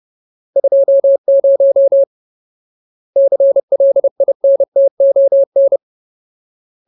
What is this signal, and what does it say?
Morse code, heard as a keyed tone, about 30 wpm, 550 Hz. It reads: 20 CLINTON
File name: Ultimate Word List / 30WPM Ultimate